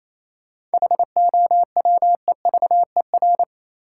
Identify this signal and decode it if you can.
Morse code, heard as a keyed tone, about 28 wpm, 700 Hz. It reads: HOWEVER
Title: Seven Letter Words - 28wpm